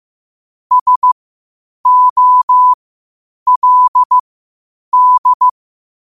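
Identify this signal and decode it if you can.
Morse code, heard as a keyed tone, about 15 wpm, spaced as for 8 wpm, 1000 Hz. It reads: SOLD
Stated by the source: Morse code quotes at 8 WPM.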